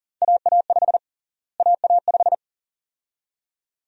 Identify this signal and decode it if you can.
Morse code, heard as a keyed tone, about 40 wpm, 700 Hz. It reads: AA5 AA5
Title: USA Call Sign Prefixes - Repeated 2x - 40wpm